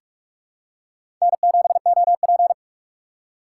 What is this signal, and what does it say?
Morse code, heard as a keyed tone, about 46 wpm, 700 Hz. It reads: N7OP
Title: USA Call Signs - 4 Characters - 6x Character Spacing - 45wpm